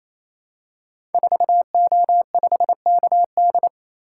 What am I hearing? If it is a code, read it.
Morse code, heard as a keyed tone, about 29 wpm, 700 Hz. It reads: 4O5KD
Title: Code Groups - 5 Letter:Numbers - 28wpm